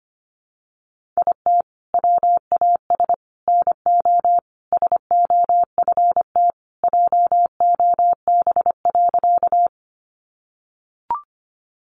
Morse code, heard as a keyed tone, about 25 wpm, 700 Hz. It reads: IT WAS NO SOFT JOB.
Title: Sentences from Top 1500 Words 25wpm